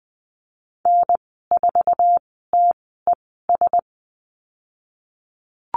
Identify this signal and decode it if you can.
Morse code, heard as a keyed tone, about 20 wpm, spaced as for 15 wpm, 700 Hz. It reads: N4TES
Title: 069 Lesson 12g - Letter d - Call Signs 20wpm